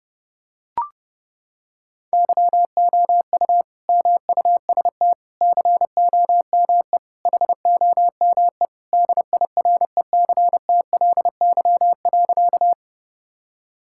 Morse code, heard as a keyed tone, about 30 wpm, 700 Hz. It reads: YOU MUST COME HOME DIRECTLY.
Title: Sentences from Top 1000 Words - Review Part 5 - 30wpm